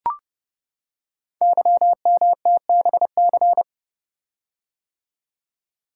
Morse code, heard as a keyed tone, about 29 wpm, 690 Hz. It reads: YMTBC
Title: Code Groups - 5 Letters - 30wpm